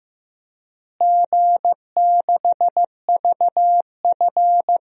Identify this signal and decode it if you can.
Morse code, heard as a keyed tone, about 15 wpm, 700 Hz. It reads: G6VF
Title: DXCC-100 Call Signs - 4 Characters - Repeated 3x - 15wpm